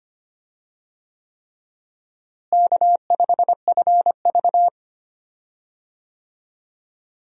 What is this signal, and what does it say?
Morse code, heard as a keyed tone, about 25 wpm, 700 Hz. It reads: K5FV
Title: USA Call Signs - 4 Characters - Repeated 3x - 25wpm